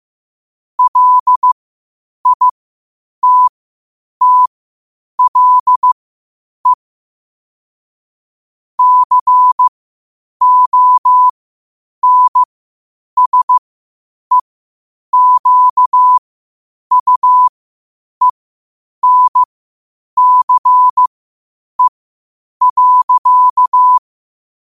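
Morse code, heard as a keyed tone, about 15 wpm, spaced as for 8 wpm, 1000 Hz. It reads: LITTLE CONSEQUENCE.
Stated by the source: Morse practice at 8 WPM from Morse Resource.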